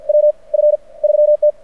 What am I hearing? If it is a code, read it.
Morse code, heard as a keyed tone, about 53 wpm, spaced as for 30 wpm, 600 Hz. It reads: WF3T
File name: archived / CRSnet